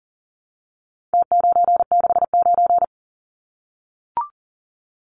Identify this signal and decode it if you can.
Morse code, heard as a keyed tone, about 40 wpm, 700 Hz. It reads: T969
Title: Numbers - Contest Style - 4-Digit 40wpm